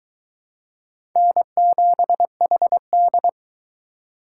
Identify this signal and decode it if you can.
Morse code, heard as a keyed tone, about 22 wpm, 700 Hz. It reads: N7HD